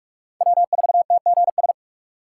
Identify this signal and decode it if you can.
Morse code, heard as a keyed tone, about 48 wpm, 700 Hz. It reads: W4TGS